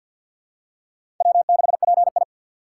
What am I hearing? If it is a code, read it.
Morse code, heard as a keyed tone, about 52 wpm, 700 Hz. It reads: W6PI
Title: USA Call Signs - 2x Character Spacing - 50wpm